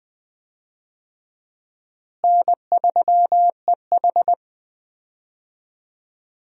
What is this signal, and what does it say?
Morse code, heard as a keyed tone, about 20 wpm, 700 Hz. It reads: N3EH
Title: USA Call Signs - 4 Characters - Repeated 3x - 20wpm